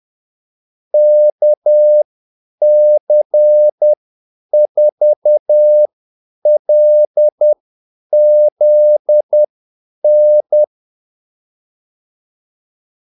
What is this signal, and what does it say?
Morse code, heard as a keyed tone, about 10 wpm, spaced as for 8 wpm, 600 Hz. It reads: KC4LZN
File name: Callsigns-10-8wpm